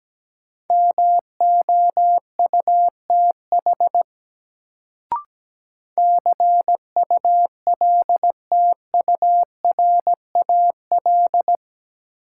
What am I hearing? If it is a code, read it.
Morse code, heard as a keyed tone, about 17 wpm, 700 Hz. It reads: MOUTH CULTURAL
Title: Sentences from Top 1500 Words - New Word Drill Alt - 17wpm